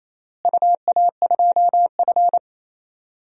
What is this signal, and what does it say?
Morse code, heard as a keyed tone, about 27 wpm, 700 Hz. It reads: UA2F
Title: DXCC-100 Call Signs - 4 Characters - Repeated 3x - 28wpm